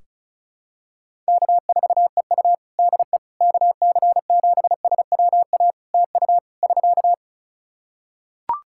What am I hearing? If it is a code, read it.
Morse code, heard as a keyed tone, about 36 wpm, 700 Hz. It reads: K4EU DE KC7SWA TU <SK>
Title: Practice QSO - Exchange 4 - Ending - 35wpm